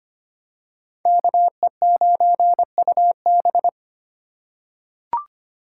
Morse code, heard as a keyed tone, about 25 wpm, 700 Hz. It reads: KE9UB
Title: USA Call Signs - 5 Characters - 8x Character Spacing - 25wpm